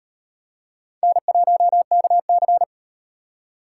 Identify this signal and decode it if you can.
Morse code, heard as a keyed tone, about 35 wpm, 700 Hz. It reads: N1KC